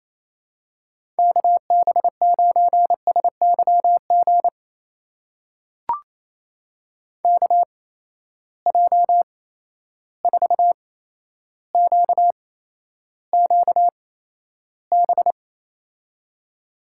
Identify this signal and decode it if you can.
Morse code, heard as a keyed tone, about 28 wpm, spaced as for 12 wpm, 700 Hz. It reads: KB9SYG KJ4QQB
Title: USA Call Signs - 6 Characters - 8x Character Spacing - 28wpm